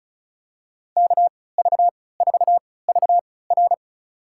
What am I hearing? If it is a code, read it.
Morse code, heard as a keyed tone, about 36 wpm, spaced as for 20 wpm, 700 Hz. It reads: KV4VR
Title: USA Call Signs - 5 Characters - 3x Character Spacing - 35wpm